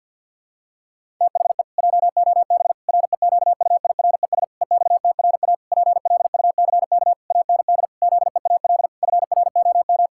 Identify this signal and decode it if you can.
Morse code, heard as a keyed tone, about 50 wpm, 700 Hz. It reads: THE JOB REQUIRES EXTRA PLUCK AND ZEAL FROM
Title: Pangram Challenge v2 - 50wpm